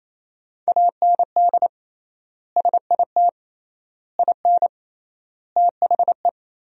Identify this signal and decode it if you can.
Morse code, heard as a keyed tone, about 28 wpm, 700 Hz. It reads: AND SIT IN THE
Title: 3x Word-Spacing - Sentences from Top 500 Words - 28wpm